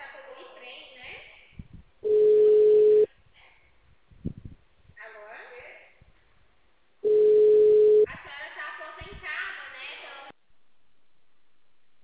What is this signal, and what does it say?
Ringback tone (Europe, 425 Hz), with speech.
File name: out- -2044- - - .27492